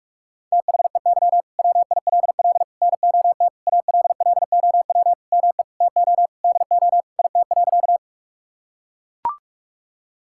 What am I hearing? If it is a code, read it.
Morse code, heard as a keyed tone, about 42 wpm, 700 Hz. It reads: THEY WILL NOT ALLOW ME TO DO IT.
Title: Sentences from Top 1000 Words - Review Part 5 - 45wpm